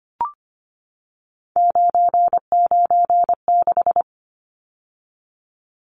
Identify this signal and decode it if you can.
Morse code, heard as a keyed tone, about 25 wpm, 700 Hz. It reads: 996
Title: Numbers - 3-Digit 25wpm